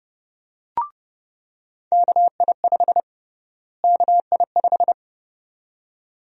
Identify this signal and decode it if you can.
Morse code, heard as a keyed tone, about 30 wpm, 700 Hz. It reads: KI5 KI5
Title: USA Call Sign Prefixes - Repeated 2x - 30wpm